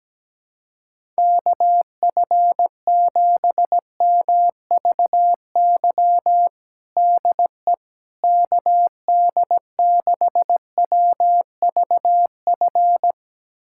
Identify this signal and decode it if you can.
Morse code, heard as a keyed tone, about 17 wpm, 700 Hz. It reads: KF7MVY DE KD6WVF
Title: USA Callsign DE Callsign - 6 Characters - 17wpm